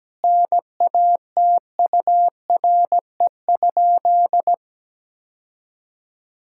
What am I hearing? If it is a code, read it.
Morse code, heard as a keyed tone, about 17 wpm, 700 Hz. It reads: NATURE?
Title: Sentences from Top 1000 Words - Review Part 6 - 17wpm